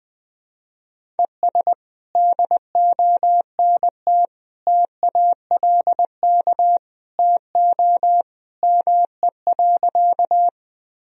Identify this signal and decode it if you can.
Morse code, heard as a keyed tone, about 20 wpm, 700 Hz. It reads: ES DONT TALK TO ME.
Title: Sentences from Top 400 Words and Abbreviations 20wpm